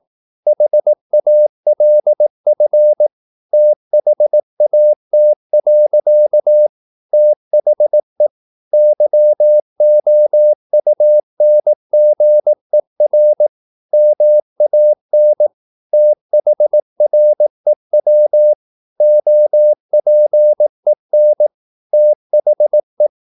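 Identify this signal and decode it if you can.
Morse code, heard as a keyed tone, about 18 wpm, 600 Hz. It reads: HALF THAT. THE YOUNGER MAN THREW OPEN THE
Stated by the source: Morse practice at 18 wpm, English text, from CW med Gnister.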